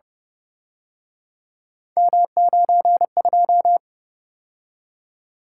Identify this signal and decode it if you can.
Morse code, heard as a keyed tone, about 29 wpm, 700 Hz. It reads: M92